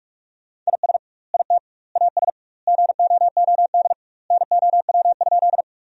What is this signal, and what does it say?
Morse code, heard as a keyed tone, about 46 wpm, 700 Hz. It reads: IS IT AS GOOD NOW?
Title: 2x Word-Spacing - Sentences from Top 200 Words - 45wpm